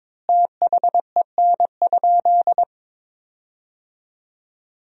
Morse code, heard as a keyed tone, about 22 wpm, 700 Hz. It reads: THEN?
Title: 3x Word-Spacing - Sentences from Top 100 Words - 22wpm